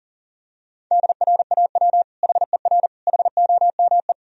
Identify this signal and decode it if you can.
Morse code, heard as a keyed tone, about 40 wpm, 700 Hz. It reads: DRAW HER HOME
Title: Sentences from Top 1000 Words - Review Part 1 - 40wpm